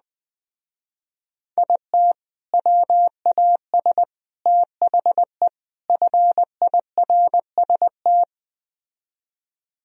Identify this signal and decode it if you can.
Morse code, heard as a keyed tone, about 21 wpm, 700 Hz. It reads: IT WAS THE FIRST